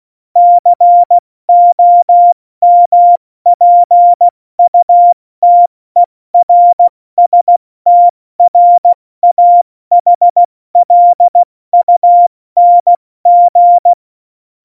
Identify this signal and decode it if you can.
Morse code, heard as a keyed tone, about 16 wpm, 700 Hz. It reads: COMPUTERSTRAHLUNG
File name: Morsekurs / Witze / Witz 08. Januar 2019, 70 F80